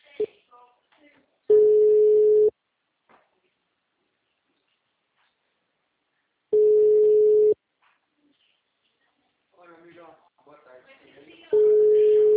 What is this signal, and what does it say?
Ringback tone (Europe, 425 Hz), with speech.
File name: out- -2011- - - .88217